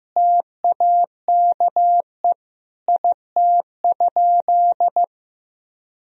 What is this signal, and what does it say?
Morse code, heard as a keyed tone, about 15 wpm, 700 Hz. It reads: TAKE IT?
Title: Sentences from Top 300 Words and Abbreviations 15wpm